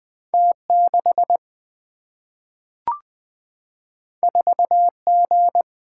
Morse code, heard as a keyed tone, about 20 wpm, 700 Hz. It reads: T6 4G
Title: Single-Letter Single-Number 20wpm